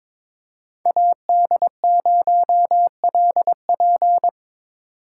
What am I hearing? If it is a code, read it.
Morse code, heard as a keyed tone, about 25 wpm, 700 Hz. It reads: AD0LP
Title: Speed Racing - USA 2x2 Call Signs 22wpm